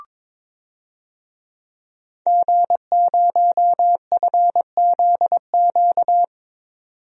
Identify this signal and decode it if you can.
Morse code, heard as a keyed tone, about 21 wpm, 700 Hz. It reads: G0FZQ
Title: DXCC-100 Call Signs - 5 Characters - 22wpm